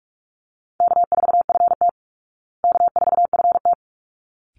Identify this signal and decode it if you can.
Morse code, heard as a keyed tone, about 44 wpm, 700 Hz. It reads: K4FT K4FT
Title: Call Signs Repeated 3x - 45wpm